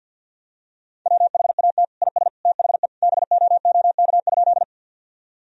Morse code, heard as a keyed tone, about 46 wpm, 700 Hz. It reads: WHAT IS THE BOOK?
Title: Sentences from Top 1000 Words - Review Part 1 - 50wpm